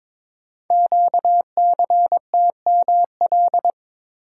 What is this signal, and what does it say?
Morse code, heard as a keyed tone, about 22 wpm, 700 Hz. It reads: QCTML
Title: Code Groups - 5 Letters - 22wpm